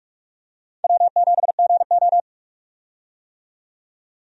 Morse code, heard as a keyed tone, about 41 wpm, 700 Hz. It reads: W7GO
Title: USA Call Signs - 4 Characters - 45wpm